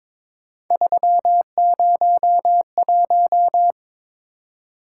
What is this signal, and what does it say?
Morse code, heard as a keyed tone, about 23 wpm, 700 Hz. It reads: 301
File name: practice / 22wpm / Three Letter-Number Combinations - 22wpm